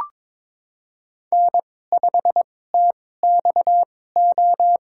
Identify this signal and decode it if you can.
Morse code, heard as a keyed tone, about 22 wpm, spaced as for 16 wpm, 700 Hz. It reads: N5TXO
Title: USA Call Signs - 5 Characters - 2x Character Spacing - 22wpm